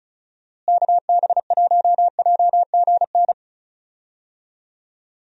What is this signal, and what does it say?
Morse code, heard as a keyed tone, about 36 wpm, 700 Hz. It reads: KB1JGN
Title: USA Call Signs - 6 Characters - Repeated 3x - 35wpm